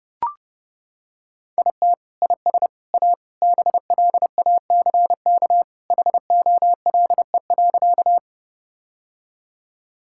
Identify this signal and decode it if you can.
Morse code, heard as a keyed tone, about 30 wpm, 700 Hz. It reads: IT IS A BLACK HOLE.
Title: Sentences from Top 1600 Words 30wpm